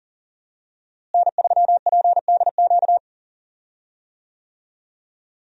Morse code, heard as a keyed tone, about 40 wpm, 700 Hz. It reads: N3PDQ
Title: USA Call Signs - 5 Characters - Repeated 2x - 40wpm